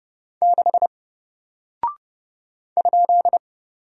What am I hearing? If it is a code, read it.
Morse code, heard as a keyed tone, about 30 wpm, 700 Hz. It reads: 6 ?
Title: Single Character (Letter:Number:Punctuation:Prosign) 30wpm